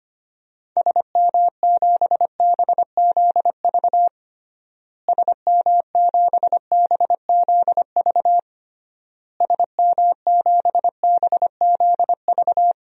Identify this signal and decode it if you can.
Morse code, heard as a keyed tone, about 25 wpm, 700 Hz. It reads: SM7BZV SM7BZV SM7BZV
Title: DXCC-100 Call Signs - 6 Characters - Repeated 3x - 25wpm